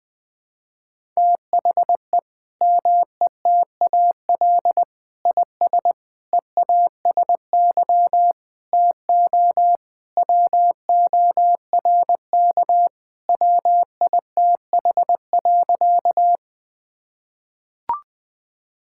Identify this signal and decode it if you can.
Morse code, heard as a keyed tone, about 20 wpm, 700 Hz. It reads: THE METAL IS EASY TO WORK WITH.
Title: Sentences from Top 1000 Words - Review Part 6 - 20wpm